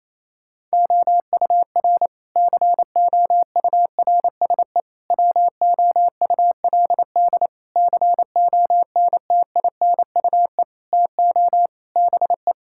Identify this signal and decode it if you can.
Morse code, heard as a keyed tone, about 28 wpm, 700 Hz. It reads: OUR COURSE WOULD CONTINUE TO BE
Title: Sentences from Top 1000 Words - Review Part 6 - 28wpm